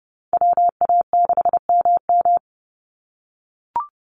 Morse code, heard as a keyed tone, about 30 wpm, 700 Hz. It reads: WA6MM
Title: NA SOTA Activator Callsigns 30wpm